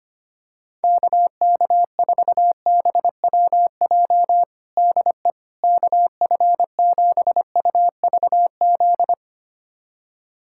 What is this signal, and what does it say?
Morse code, heard as a keyed tone, about 25 wpm, 700 Hz. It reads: KK4BWJ DE KF7UVZ
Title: USA Callsign DE Callsign - 6 Characters - 25wpm